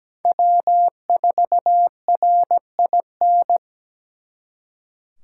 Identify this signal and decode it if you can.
Morse code, heard as a keyed tone, about 20 wpm, 700 Hz. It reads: W4RIN
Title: Speed Racing - USA 1x3 Call Signs 17wpm